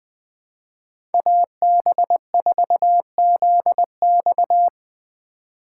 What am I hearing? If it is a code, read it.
Morse code, heard as a keyed tone, about 20 wpm, 700 Hz. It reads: AB4ZX
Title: USA Call Signs - 5 Characters - 3x Character Spacing - 20wpm